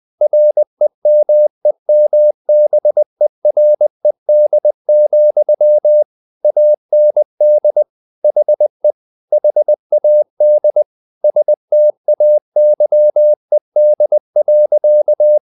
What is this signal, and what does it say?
Morse code, heard as a keyed tone, about 20 wpm, 600 Hz. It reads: REMEMBERED, AND HE HAD STAYED.
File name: the old man and the sea / the old man and the sea part 1 20wpm